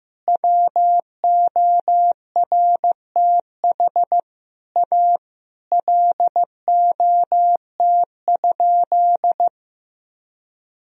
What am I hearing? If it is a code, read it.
Morse code, heard as a keyed tone, about 15 wpm, 700 Hz. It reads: WORTH A LOT?
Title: Sentences from Top 1000 Words - Review Part 4 - 15wpm